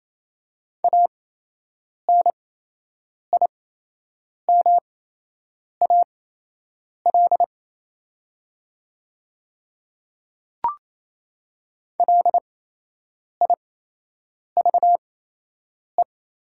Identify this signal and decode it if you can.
Morse code, heard as a keyed tone, about 28 wpm, spaced as for 8 wpm, 700 Hz. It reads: ANIMAL LIVE
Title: 8x Character Spacing - Top 300 Words - 28wpm